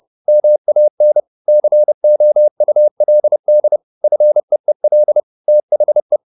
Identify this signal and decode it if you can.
Morse code, heard as a keyed tone, about 30 wpm, 600 Hz. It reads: MAN COULD FEEL THE
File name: the old man and the sea / the old man and the sea 1 30wpm